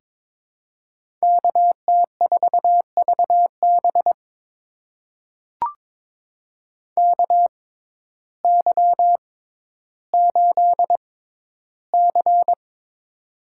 Morse code, heard as a keyed tone, about 22 wpm, spaced as for 12 wpm, 700 Hz. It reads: KT4VB KY8C
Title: USA Call Signs - 6x Character Spacing - 22wpm